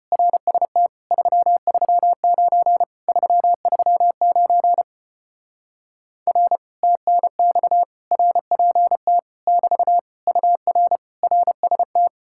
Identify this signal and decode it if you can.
Morse code, heard as a keyed tone, about 33 wpm, 700 Hz. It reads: RST 339 339 R TNX RPT <BT> UR RST